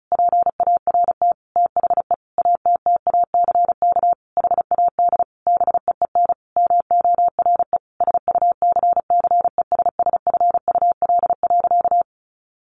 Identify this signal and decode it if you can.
Morse code, heard as a keyed tone, about 34 wpm, 700 Hz. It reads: PART THE ATTACK HAD BEEN MORE SUCCESSFUL.
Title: Sentences from Top 1500 Words 35wpm